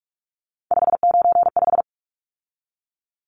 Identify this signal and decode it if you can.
Morse code, heard as a keyed tone, about 44 wpm, 700 Hz. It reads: 595
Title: Numbers - 3-Digit 45wpm